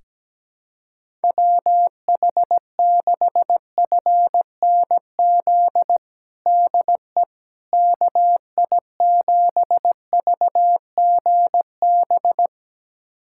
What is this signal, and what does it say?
Morse code, heard as a keyed tone, about 17 wpm, 700 Hz. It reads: WH6FNZ DE KI7VGB
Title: USA Callsign DE Callsign - 6 Characters - 17wpm